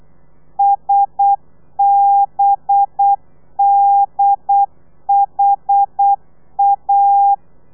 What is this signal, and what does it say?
Morse code, heard as a keyed tone, about 8 wpm, 800 Hz. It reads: SBDHA
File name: telegrafia / i2tfj / Lez4 35